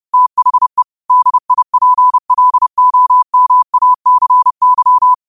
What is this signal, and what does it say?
Morse code, heard as a keyed tone, about 30 wpm, 1000 Hz. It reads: THE DIPLOMACY